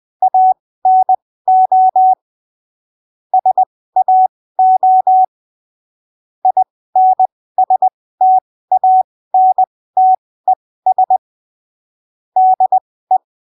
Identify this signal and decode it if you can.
Morse code, heard as a keyed tone, about 20 wpm, spaced as for 15 wpm, 750 Hz. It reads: ANO SAO INSTANTES DE
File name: arquivos / texto / texto 27 20ppm